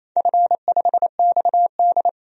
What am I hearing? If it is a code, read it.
Morse code, heard as a keyed tone, about 28 wpm, 700 Hz. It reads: F5XD
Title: DXCC-100 Call Signs - 4 Characters - 28wpm